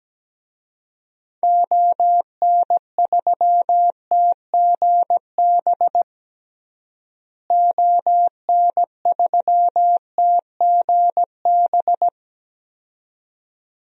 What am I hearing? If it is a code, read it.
Morse code, heard as a keyed tone, about 17 wpm, 700 Hz. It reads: ON3TGB ON3TGB
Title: DXCC-100 Call Signs - 6 Characters - Repeated 2x - 17wpm